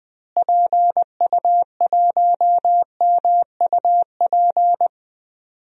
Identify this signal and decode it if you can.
Morse code, heard as a keyed tone, about 20 wpm, 700 Hz. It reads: PU1MUP
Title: DXCC-100 Call Signs - 6 Characters - Repeated 2x - 20wpm